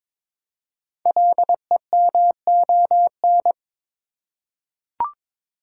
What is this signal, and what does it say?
Morse code, heard as a keyed tone, about 22 wpm, 700 Hz. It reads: LEMON
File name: 5 Letter Words - First 3 Letters Given - 22wpm